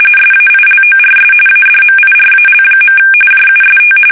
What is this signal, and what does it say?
RTTY, about 45.45 baud. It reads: SAILORS ADMIRAL. JUST
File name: fsk850 / battleshipnj